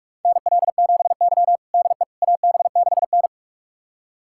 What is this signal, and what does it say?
Morse code, heard as a keyed tone, about 46 wpm, 700 Hz. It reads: NL7Y DE AB6N
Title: USA Callsign DE Callsign - 4 Characters - 45wpm